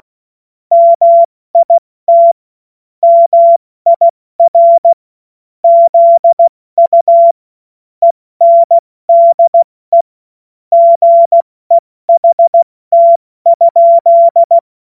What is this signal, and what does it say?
Morse code, heard as a keyed tone, about 16 wpm, 680 Hz. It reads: MIT MIR ZU ENDE GEHT?
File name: Morsekurs / Witze / Witz 26 März 2024 70 F80